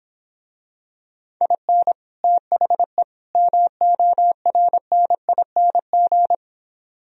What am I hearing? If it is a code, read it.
Morse code, heard as a keyed tone, about 26 wpm, 700 Hz. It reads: IN THE MORNING